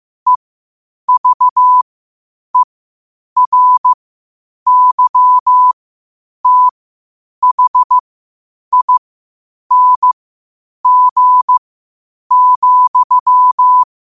Morse code, heard as a keyed tone, about 15 wpm, spaced as for 8 wpm, 1000 Hz. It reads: EVERYTHING,
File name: qotdfiles / 8wpm